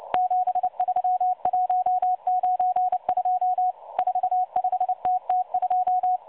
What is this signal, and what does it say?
Morse code, heard as a keyed tone, about 29 wpm, 730 Hz. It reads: 73192 45TT2
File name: M12 - 8116 kHz CW 2-11-14